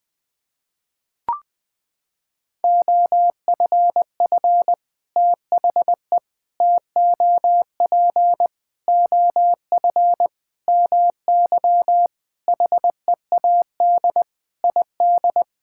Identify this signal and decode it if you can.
Morse code, heard as a keyed tone, about 20 wpm, 700 Hz. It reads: OFF THE TOP OF MY HEAD ID
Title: Sentences with 6-Word Phrases - 20wpm